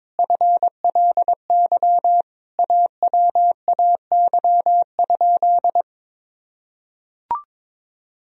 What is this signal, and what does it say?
Morse code, heard as a keyed tone, about 22 wpm, 700 Hz. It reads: FLY AWAY?
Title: Sentences from Top 1000 Words - Review Part 2 - 22wpm